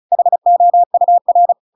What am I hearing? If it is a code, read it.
Morse code, heard as a keyed tone, about 34 wpm, 700 Hz. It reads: HOUR